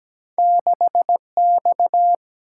Morse code, heard as a keyed tone, about 17 wpm, 700 Hz. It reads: 6X
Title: Single-Letter Single-Number 17wpm